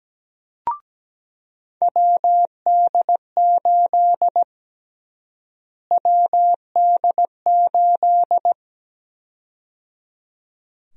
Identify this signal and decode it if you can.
Morse code, heard as a keyed tone, about 17 wpm, 700 Hz. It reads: WD8 WD8
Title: USA Call Sign Prefixes - Repeated 2x - 17wpm